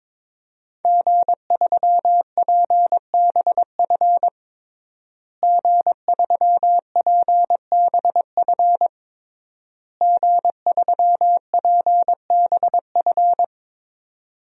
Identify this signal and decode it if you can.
Morse code, heard as a keyed tone, about 22 wpm, 700 Hz. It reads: G3PBF G3PBF G3PBF
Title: DXCC-100 Call Signs - 5 Characters - Repeated 3x - 22wpm